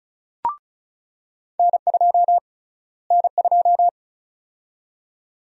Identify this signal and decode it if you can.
Morse code, heard as a keyed tone, about 36 wpm, 700 Hz. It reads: N2 N2
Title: USA Call Sign Prefixes - Repeated 2x - 35wpm